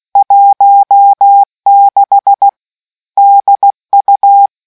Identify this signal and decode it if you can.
Morse code, heard as a keyed tone, about 16 wpm, 800 Hz. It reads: 16 DU